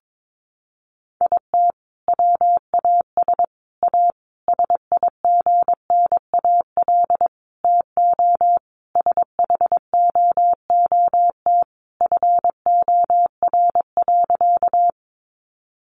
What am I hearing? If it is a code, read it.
Morse code, heard as a keyed tone, about 22 wpm, 700 Hz. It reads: IT WAS A SIGNAL TO SHOOT FOR.
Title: Sentences from Top 1400 Words 22wpm